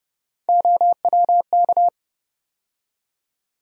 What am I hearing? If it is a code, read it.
Morse code, heard as a keyed tone, about 30 wpm, 700 Hz. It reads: OWK
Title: USA Call Sign Postfixes - 3 Letters 30wpm